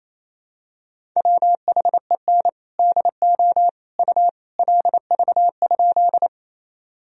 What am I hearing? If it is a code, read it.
Morse code, heard as a keyed tone, about 28 wpm, 700 Hz. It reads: WHEN DO U LV?
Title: Sentences from Top 400 Words and Abbreviations 28wpm